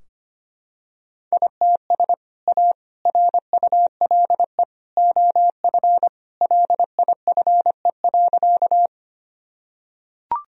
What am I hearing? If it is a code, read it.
Morse code, heard as a keyed tone, about 25 wpm, 700 Hz. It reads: ITS A RULE OF LIFE.
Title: Sentences from Top 1000 Words - Review Part 2 - 25wpm